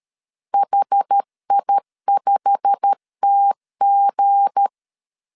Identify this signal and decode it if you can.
Morse code, heard as a keyed tone, about 13 wpm, 800 Hz. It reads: HI5TG
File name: CW / DET / 1 9 1500C S B 12wpm 3819